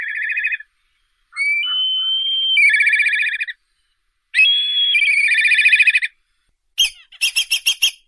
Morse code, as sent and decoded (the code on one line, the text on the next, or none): none